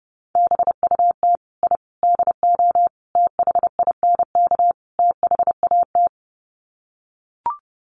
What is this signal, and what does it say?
Morse code, heard as a keyed tone, about 30 wpm, 700 Hz. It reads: BUT I DO THINK THAT
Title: Sets of 5 Words Encore2 30wpm